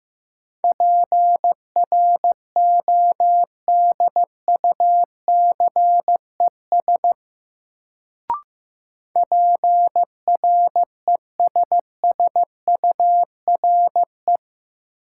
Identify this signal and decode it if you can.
Morse code, heard as a keyed tone, about 15 wpm, 700 Hz. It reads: PRODUCES PRESSURE
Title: Sentences from Top 1000 Words - Review Part 6 - New Word Drill - 15wpm